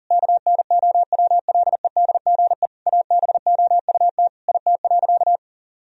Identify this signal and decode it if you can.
Morse code, heard as a keyed tone, about 40 wpm, 700 Hz. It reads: KNOWLEDGE ABOUT IT.
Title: Sentences from Top 1000 Words - Review Part 3 - 40wpm